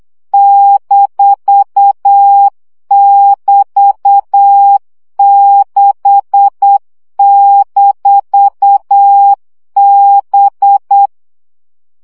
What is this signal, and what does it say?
Morse code, heard as a keyed tone, about 8 wpm, 800 Hz. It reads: -<BT>6-B
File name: CW gyakorlas b 6 egyenlo minusz 7 baud